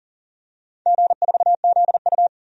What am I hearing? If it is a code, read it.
Morse code, heard as a keyed tone, about 40 wpm, 700 Hz. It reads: G4ZU
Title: DXCC-100 Call Signs - 4 Characters - 40wpm